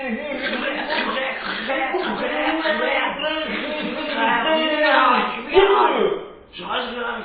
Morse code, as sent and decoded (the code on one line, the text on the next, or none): none